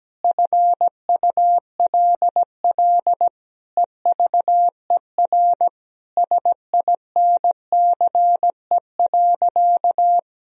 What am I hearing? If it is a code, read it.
Morse code, heard as a keyed tone, about 17 wpm, 700 Hz. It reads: FULL EVER SINCE.
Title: Sentences from Top 1000 Words - Review Part 3 - 17wpm